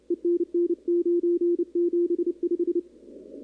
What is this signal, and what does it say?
Morse code, heard as a keyed tone, about 31 wpm, 350 Hz. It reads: ENN975
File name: contests / 2019-cw-wpx-kt5j-k5pi- -2330 0-11